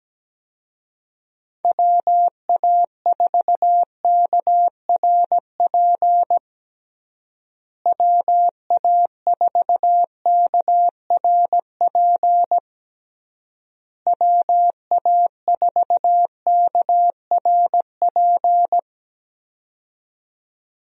Morse code, heard as a keyed tone, about 17 wpm, 700 Hz. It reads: WA4KRP WA4KRP WA4KRP
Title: USA Call Signs - 6 Characters - Repeated 3x - 17wpm